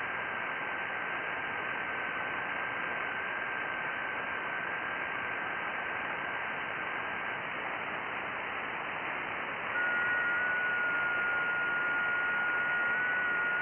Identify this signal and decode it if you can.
FT8.
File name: IC-7100